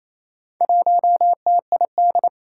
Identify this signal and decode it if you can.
Morse code, heard as a keyed tone, about 27 wpm, 700 Hz. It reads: 1TID
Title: Four Challenging Letter-Number Combinations 28wpm